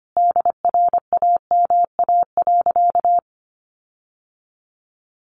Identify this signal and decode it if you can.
Morse code, heard as a keyed tone, about 25 wpm, 700 Hz. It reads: DRAMA.
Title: Sentences from Top 1500 Words 25wpm